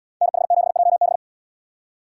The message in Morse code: NH7JB

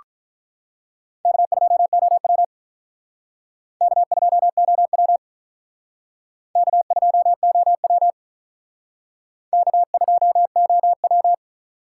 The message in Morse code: K2OW K2OW K2OW K2OW

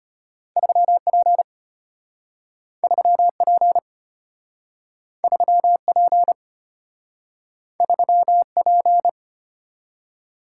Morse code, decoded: 3P 3P 3P 3P

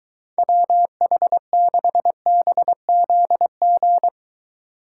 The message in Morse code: WH6BZG

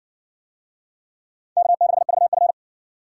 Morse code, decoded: K6VF

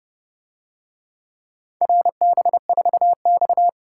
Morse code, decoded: RB4X

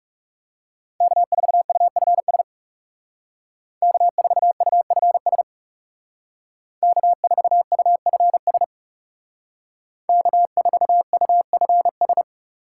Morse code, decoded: K4UFS K4UFS K4UFS K4UFS

